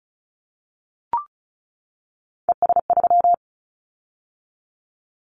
Morse code, heard as a keyed tone, about 36 wpm, 700 Hz. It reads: ES3
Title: Callsign Prefixes for Top 50 DXCC - 35wpm